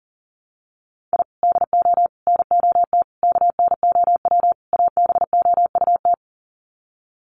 Morse code, heard as a keyed tone, about 40 wpm, 700 Hz. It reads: I DO NOT KNOW ABOUT
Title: Sets of 5 Words 40wpm